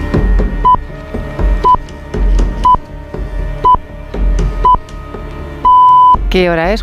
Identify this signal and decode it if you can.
Radio time signal (time pips), with speech. It is heard from 0.6 to 6.1 s.